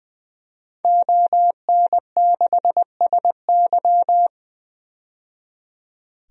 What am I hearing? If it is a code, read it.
Morse code, heard as a keyed tone, about 20 wpm, 700 Hz. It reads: ON6SY
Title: DXCC-100 Call Signs - 5 Characters - 20wpm